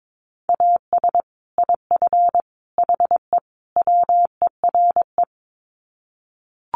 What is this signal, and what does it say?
Morse code, heard as a keyed tone, about 22 wpm, 700 Hz. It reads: AS IF HE WERE
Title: Sets of 4 Words 22wpm